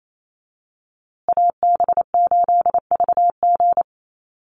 Morse code, heard as a keyed tone, about 28 wpm, 700 Hz. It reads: AB8VG
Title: USA Call Signs Repeated 3x 28wpm